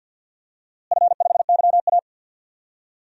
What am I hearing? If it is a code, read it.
Morse code, heard as a keyed tone, about 52 wpm, 700 Hz. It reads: F5YA
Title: DXCC-100 Call Signs - 4 Characters - Repeated 2x - 50wpm